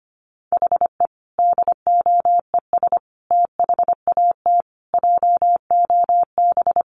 Morse code, heard as a keyed tone, about 25 wpm, 700 Hz. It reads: HE DOES THAT JOB
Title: Sentences from Top 700 Words 25wpm